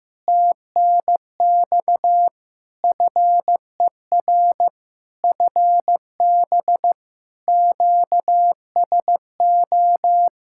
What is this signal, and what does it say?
Morse code, heard as a keyed tone, about 15 wpm, 700 Hz. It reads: TNX FER FB QSO
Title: Practice QSO - Exchange 4 - Ending - 15wpm